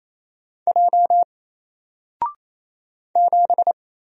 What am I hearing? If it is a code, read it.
Morse code, heard as a keyed tone, about 27 wpm, 700 Hz. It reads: J 7
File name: Single-Letter-Number-28wpm